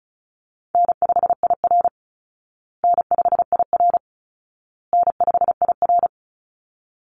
Morse code, heard as a keyed tone, about 34 wpm, 700 Hz. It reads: N5IR N5IR N5IR
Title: Call Signs Repeated 2x - 35wpm